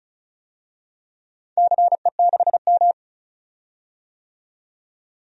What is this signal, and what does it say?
Morse code, heard as a keyed tone, about 36 wpm, 690 Hz. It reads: CE6M
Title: DXCC-100 Call Signs - 4 Characters - 35wpm